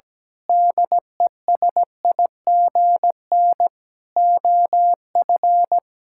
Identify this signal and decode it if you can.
Morse code, heard as a keyed tone, about 17 wpm, 700 Hz. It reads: DESIGN OF
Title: Sentences from Top 1000 Words - Review Part 6 - 17wpm